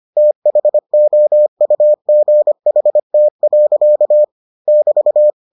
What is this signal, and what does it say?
Morse code, heard as a keyed tone, about 25 wpm, 600 Hz. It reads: THOUGHT. <BT>